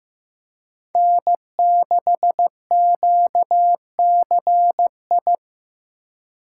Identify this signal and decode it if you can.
Morse code, heard as a keyed tone, about 16 wpm, 700 Hz. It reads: N6QCI
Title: USA Call Signs - 5 Characters - 4x Character Spacing - 15wpm